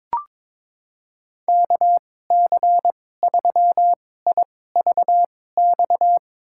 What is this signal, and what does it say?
Morse code, heard as a keyed tone, about 22 wpm, spaced as for 16 wpm, 700 Hz. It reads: KC3IVX
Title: USA Call Signs - 6 Characters - 2x Character Spacing - 22wpm